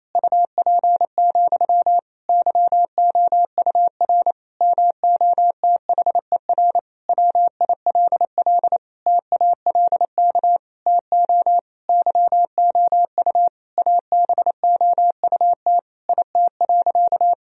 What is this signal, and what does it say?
Morse code, heard as a keyed tone, about 28 wpm, 700 Hz. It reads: UP, YOUR MOTHER WILL TALK TO YOU ABOUT IT.
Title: Sentences from Top 1000 Words - Review Part 1 - 28wpm